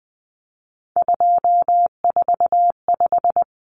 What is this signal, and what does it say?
Morse code, heard as a keyed tone, about 20 wpm, 700 Hz. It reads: 245
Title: Common Cut Numbers - 3-Digit 20wpm